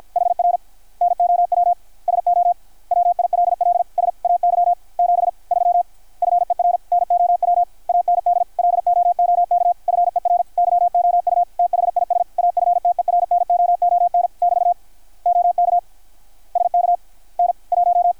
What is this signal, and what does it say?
Morse code, heard as a keyed tone, about 51 wpm, 700 Hz. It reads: HR NOW SO WILL SAY 73 FER NOW AND LOOK FER YOU THIS AFTERNOON <BT> OK SK N1
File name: archived / CRSnet